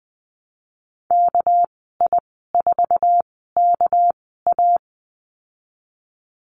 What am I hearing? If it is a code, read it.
Morse code, heard as a keyed tone, about 20 wpm, spaced as for 15 wpm, 700 Hz. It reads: KI4KA
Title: 231 Lesson 37g - Letter x - Call Signs 20wpm